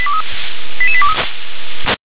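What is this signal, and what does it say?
Five-tone selective call (ZVEI), heard as quick runs of five tones, tones of about 70 ms. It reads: 88823
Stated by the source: ZVEI five-tone call received by SDR.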